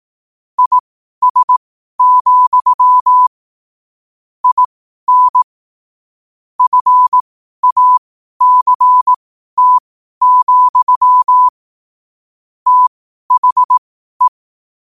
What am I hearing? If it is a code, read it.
Morse code, heard as a keyed tone, about 18 wpm, spaced as for 12 wpm, 1000 Hz. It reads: IS, IN FACT, THE
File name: qotdfiles / 12wpm